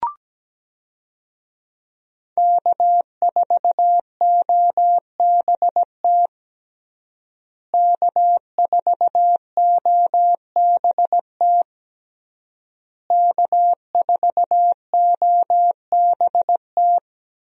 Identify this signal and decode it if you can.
Morse code, heard as a keyed tone, about 17 wpm, 700 Hz. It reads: K4OBT K4OBT K4OBT
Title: USA Call Signs - 5 Characters - Repeated 3x - 17wpm